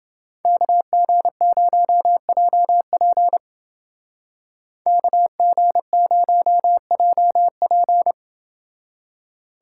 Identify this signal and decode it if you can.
Morse code, heard as a keyed tone, about 28 wpm, 700 Hz. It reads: KG0JP KG0JP